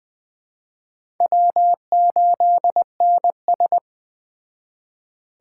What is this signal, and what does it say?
Morse code, heard as a keyed tone, about 20 wpm, 700 Hz. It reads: W8NS
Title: Speed Racing - USA 1x2 Call Signs 17wpm